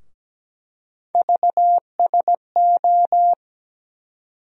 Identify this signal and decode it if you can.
Morse code, heard as a keyed tone, about 17 wpm, 700 Hz. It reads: VSO
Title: USA Call Sign Postfixes - 3 Letters 17wpm